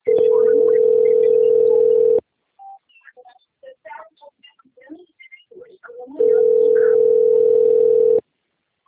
Ringback tone (North America), 440 + 480 Hz.
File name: out- - - - - .56112